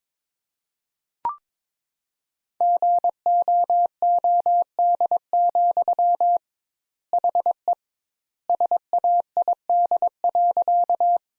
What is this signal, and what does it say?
Morse code, heard as a keyed tone, about 22 wpm, 700 Hz. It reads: GOOD, HE SAID.
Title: Sentences Top 200 Words - 22wpm